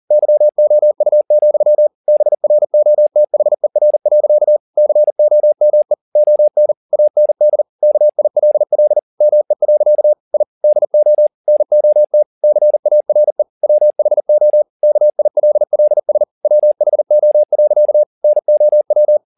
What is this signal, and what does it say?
Morse code, heard as a keyed tone, about 40 wpm, 600 Hz. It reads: YOU, BROTHER. COME ON AND KILL ME. I DO NOT CARE WHO KILLS WHO. NOW
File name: the old man and the sea 2 40wpm